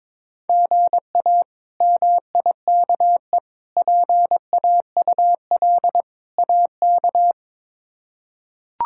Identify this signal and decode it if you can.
Morse code, heard as a keyed tone, about 22 wpm, 700 Hz. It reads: GA MIKE PAUL AK
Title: K1USN SST QSO Practice - 22wpm